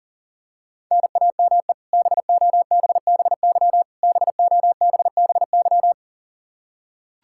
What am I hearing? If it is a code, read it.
Morse code, heard as a keyed tone, about 40 wpm, 700 Hz. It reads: NAME BOBBY BOBBY
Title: Practice QSO - Exchange 1 - Beginning - 40wpm